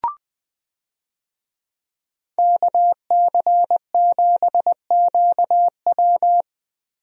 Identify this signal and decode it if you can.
Morse code, heard as a keyed tone, about 21 wpm, 700 Hz. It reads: KC7QW
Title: USA Call Signs - 5 Characters - Repeated 3x - 20wpm